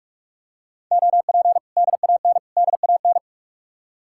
Morse code, heard as a keyed tone, about 50 wpm, 700 Hz. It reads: OP DAN DAN